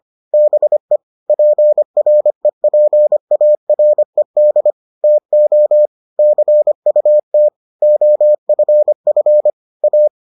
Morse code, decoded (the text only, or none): BE PREPARED TO CUT OFF A